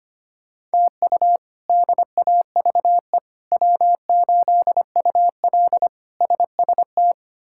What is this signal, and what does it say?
Morse code, heard as a keyed tone, about 25 wpm, 700 Hz. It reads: TU DAVE W8UL SST
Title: K1USN SST QSO Practice - 25wpm